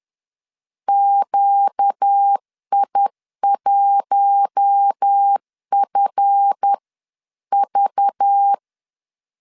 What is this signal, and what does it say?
Morse code, heard as a keyed tone, about 10 wpm, spaced as for 9 wpm, 800 Hz. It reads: QI1FV
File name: LEZIONI / 7 GF9Z / 7 GF 10wpm 2514